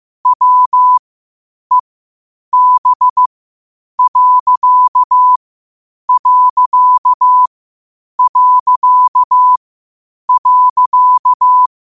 Morse code, heard as a keyed tone, about 15 wpm, spaced as for 8 wpm, 1000 Hz. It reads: WEB....
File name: qotdfiles / 8wpm